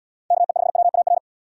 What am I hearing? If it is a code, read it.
Morse code, heard as a keyed tone, about 69 wpm, 700 Hz. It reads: KE5WAR